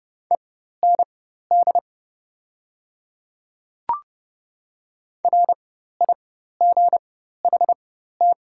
Morse code, heard as a keyed tone, about 31 wpm, spaced as for 14 wpm, 700 Hz. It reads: END RIGHT
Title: 4x Character Spacing - Top 200 Words - 30wpm